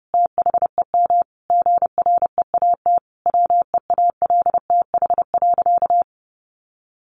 Morse code, heard as a keyed tone, about 30 wpm, 700 Hz. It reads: THEM GREAT WEALTH.
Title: Sentences from Top 1300 Words 30wpm